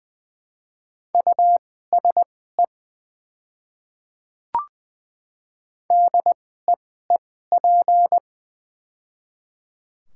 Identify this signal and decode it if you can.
Morse code, heard as a keyed tone, about 20 wpm, spaced as for 14 wpm, 690 Hz. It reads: USE DEEP
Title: 2x Character Spacing - Top 500 Words - 20wpm